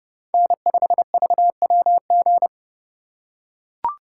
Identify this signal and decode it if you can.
Morse code, heard as a keyed tone, about 30 wpm, 700 Hz. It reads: N5VWG